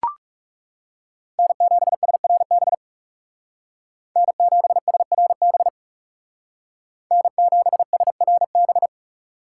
Morse code, heard as a keyed tone, about 38 wpm, 700 Hz. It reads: N7SRB N7SRB N7SRB